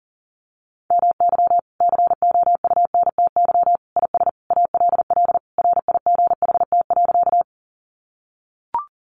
Morse code, heard as a keyed tone, about 40 wpm, 700 Hz. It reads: MY COUNTY IS ALL RIGHT.
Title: Sentences from Top 1200 Words 40wpm